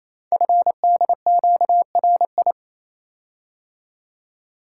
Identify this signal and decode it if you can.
Morse code, heard as a keyed tone, about 27 wpm, 700 Hz. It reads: FDQRI
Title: Code Groups - 5 Letters - 28wpm